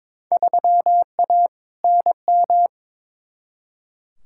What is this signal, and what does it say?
Morse code, heard as a keyed tone, about 22 wpm, 700 Hz. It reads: 3A NM
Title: Field Day Exchanges Encore - 22wpm